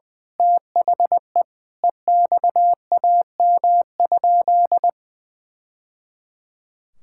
Morse code, heard as a keyed tone, about 21 wpm, 700 Hz. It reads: THE EXAM?
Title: Sentences from Top 1000 Words - Review Part 5 - 20wpm